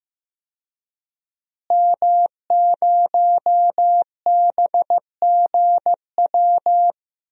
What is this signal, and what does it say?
Morse code, heard as a keyed tone, about 15 wpm, 700 Hz. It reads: M0BGW
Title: DXCC-100 Call Signs - 5 Characters - Repeated 2x - 15wpm